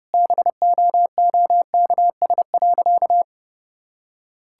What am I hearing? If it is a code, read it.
Morse code, heard as a keyed tone, about 32 wpm, 700 Hz. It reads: BOOKS.
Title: Sentences from Top 1000 Words - Review Part 5 - 30wpm